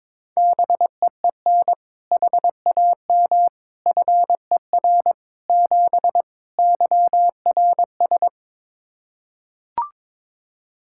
Morse code, heard as a keyed tone, about 21 wpm, 700 Hz. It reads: BEEN HAM FER 7 YRS
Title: K1USN SST QSO Practice 22wpm